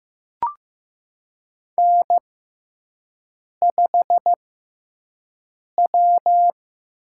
Morse code, heard as a keyed tone, about 16 wpm, spaced as for 5 wpm, 700 Hz. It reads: N5W
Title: USA Call Signs - 4 Characters - 6x Character Spacing - 15wpm